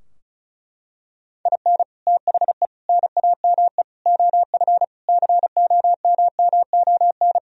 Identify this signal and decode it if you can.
Morse code, heard as a keyed tone, about 34 wpm, 700 Hz. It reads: IN THE NAME OF COMMON
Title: Sentences from Top 1000 Words - Review Part 4 - 35wpm